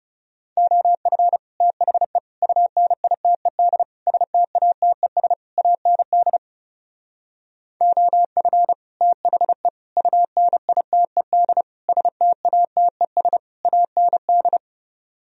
Morse code, heard as a keyed tone, about 34 wpm, 700 Hz. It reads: OF THE UNITED STATES AND OF THE UNITED STATES AND
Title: Speed-Racing - Sets of 5 Words 30wpm